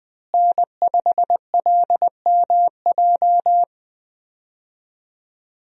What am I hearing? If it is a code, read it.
Morse code, heard as a keyed tone, about 20 wpm, 700 Hz. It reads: N5LMJ
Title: USA Call Signs - 5 Characters - Repeated 3x - 20wpm